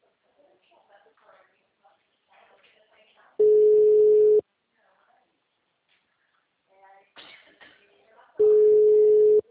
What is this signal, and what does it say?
Ringback tone (Europe, 425 Hz), with speech.